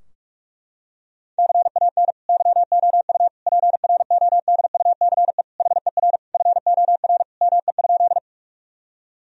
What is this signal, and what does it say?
Morse code, heard as a keyed tone, about 41 wpm, 700 Hz. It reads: CAN YOU PRODUCE HER FOR ME?
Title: Sentences from Top 1000 Words - Review Part 3 - 45wpm